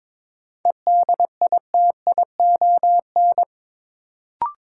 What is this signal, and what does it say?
Morse code, heard as a keyed tone, about 22 wpm, 700 Hz. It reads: EDITION
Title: Sets of 5 Words - New Word Drill - 22wpm